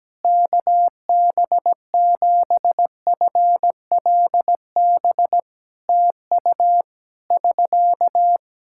Morse code, heard as a keyed tone, about 17 wpm, 700 Hz. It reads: KB7FLB TU <SK>
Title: Practice QSO - Exchange 4 - Ending - 17wpm